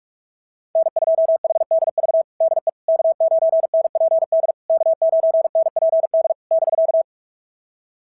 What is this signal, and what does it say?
Morse code, heard as a keyed tone, about 44 wpm, 660 Hz. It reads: N2HDV DE K9NPD K9NPD <BK>
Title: SOTA QSO Practice 45wpm